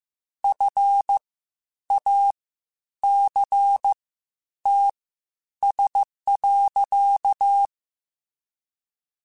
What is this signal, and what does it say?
Morse code, heard as a keyed tone, about 15 wpm, spaced as for 9 wpm, 800 Hz. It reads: FACTS.
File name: Quotes Spaced / Quotes Spaced-mp3s / IgnoringFacts sp @13